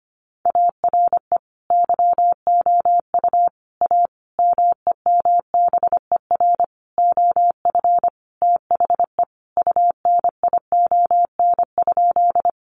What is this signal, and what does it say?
Morse code, heard as a keyed tone, about 25 wpm, 700 Hz. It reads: ARE YOU A MEMBER OF THE UNION?
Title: Sentences from Top 1400 Words 25wpm